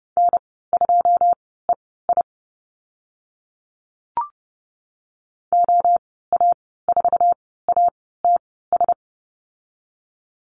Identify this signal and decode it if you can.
Morse code, heard as a keyed tone, about 30 wpm, spaced as for 17 wpm, 700 Hz. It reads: N2EI OA4ATS
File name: Lesson 14g - Number 2 - Call Signs 30wpm